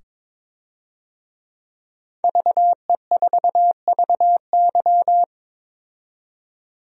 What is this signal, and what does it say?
Morse code, heard as a keyed tone, about 22 wpm, 700 Hz. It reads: VE4VY
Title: DXCC-100 Call Signs - 5 Characters - 22wpm